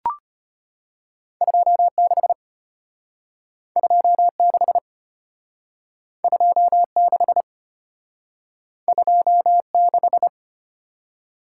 Morse code, heard as a keyed tone, about 32 wpm, 700 Hz. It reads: 26 26 26 26